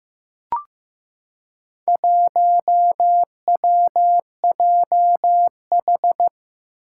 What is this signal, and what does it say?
Morse code, heard as a keyed tone, about 15 wpm, 700 Hz. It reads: 1WJH
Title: Four Challenging Letter-Number Combinations 15wpm